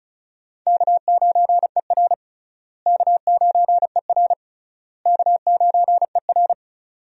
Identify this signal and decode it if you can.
Morse code, heard as a keyed tone, about 36 wpm, 700 Hz. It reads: K9ER K9ER K9ER
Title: USA Call Signs - 4 Characters - Repeated 3x - 35wpm